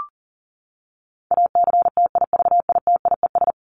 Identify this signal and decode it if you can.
Morse code, heard as a keyed tone, about 37 wpm, 700 Hz. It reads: ACTIVITIES
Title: ES:S Suffix - Top 100 Words with Base Words 40wpm